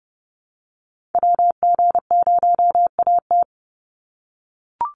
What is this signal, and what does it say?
Morse code, heard as a keyed tone, about 30 wpm, 700 Hz. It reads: WG0AT
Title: NA SOTA Activator Callsigns 30wpm